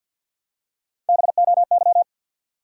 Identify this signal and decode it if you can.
Morse code, heard as a keyed tone, about 53 wpm, 700 Hz. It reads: BOY
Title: Three Letter Words - 50wpm